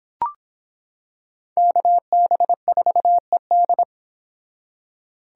Morse code, heard as a keyed tone, about 26 wpm, 700 Hz. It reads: KB4ED